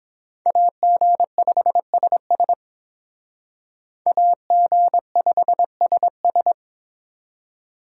Morse code, heard as a keyed tone, about 28 wpm, 700 Hz. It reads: AG5SS AG5SS